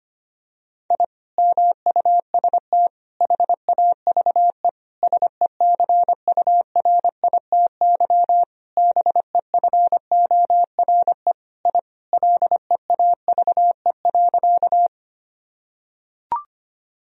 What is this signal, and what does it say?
Morse code, heard as a keyed tone, about 25 wpm, 700 Hz. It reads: I MUST HAVE SECURITY BEFORE I LEAVE.
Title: Sentences from Top 1000 Words - Review Part 4 - 25wpm